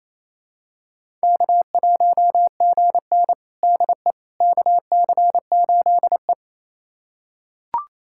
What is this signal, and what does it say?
Morse code, heard as a keyed tone, about 28 wpm, 700 Hz. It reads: K1GN DE KC8E
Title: USA Callsign DE Callsign - 4 Characters - 28wpm